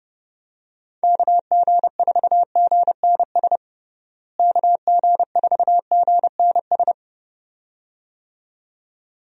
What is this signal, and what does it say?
Morse code, heard as a keyed tone, about 32 wpm, 700 Hz. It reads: KG4GNS KG4GNS
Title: USA Call Signs - 6 Characters - Repeated 2x - 30wpm